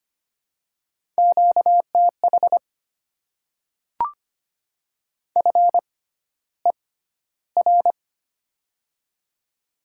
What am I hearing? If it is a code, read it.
Morse code, heard as a keyed tone, about 25 wpm, spaced as for 13 wpm, 700 Hz. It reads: QTH FER
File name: Top 20 QSO Elements - 6x Character Spacing - 25wpm